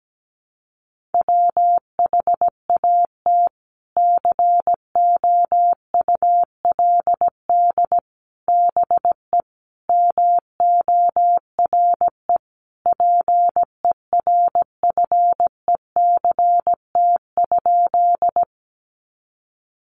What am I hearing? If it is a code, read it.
Morse code, heard as a keyed tone, about 17 wpm, 700 Hz. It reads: WHAT COULD BE MORE PERFECT?
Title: Sentences from Top 1000 Words 17wpm